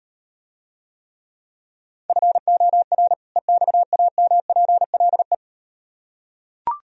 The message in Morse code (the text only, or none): FOR EXAMPLE